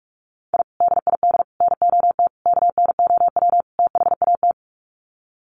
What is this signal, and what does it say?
Morse code, heard as a keyed tone, about 44 wpm, 700 Hz. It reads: I DID NOT KNOW THAT
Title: Sets of 5 Words 45wpm